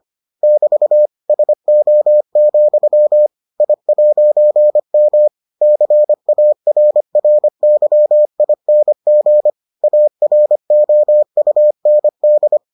Morse code, heard as a keyed tone, about 25 wpm, 600 Hz. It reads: <BT> SO, I'M CARRYING AROUND